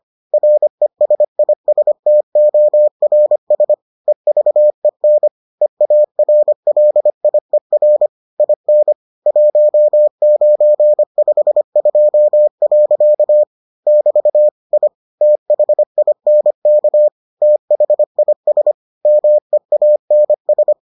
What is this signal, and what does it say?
Morse code, heard as a keyed tone, about 25 wpm, 600 Hz. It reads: RESISTORS EVEN EARLIER IN 1952. <BT> I THINK THIS MEANS